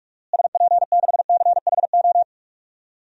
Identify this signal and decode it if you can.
Morse code, heard as a keyed tone, about 45 wpm, 700 Hz. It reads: SP6CHO
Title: DXCC-100 Call Signs - 6 Characters - Repeated 3x - 45wpm